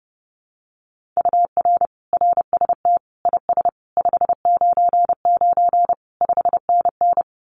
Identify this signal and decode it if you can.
Morse code, heard as a keyed tone, about 30 wpm, 700 Hz. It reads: UR RST IS 599 5NN
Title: Common QSO Phrases Encore 30wpm